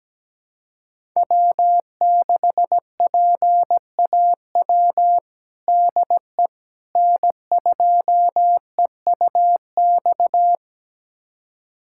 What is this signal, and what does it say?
Morse code, heard as a keyed tone, about 17 wpm, 700 Hz. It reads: W6PAW DE N2EUX
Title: USA Callsign DE Callsign - 5 Characters - 17wpm